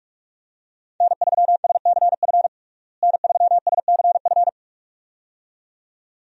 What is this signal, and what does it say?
Morse code, heard as a keyed tone, about 46 wpm, 700 Hz. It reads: N3SCF N3SCF
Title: USA Call Signs - 5 Characters - Repeated 2x - 45wpm